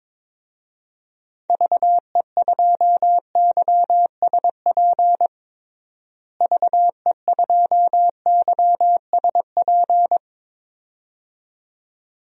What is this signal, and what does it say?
Morse code, heard as a keyed tone, about 21 wpm, 700 Hz. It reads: VE2YSP VE2YSP
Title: DXCC-100 Call Signs - 6 Characters - Repeated 2x - 22wpm